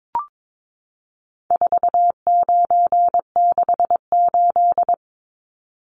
Morse code, heard as a keyed tone, about 22 wpm, 700 Hz. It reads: 4968
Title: Numbers - 4-Digit 22wpm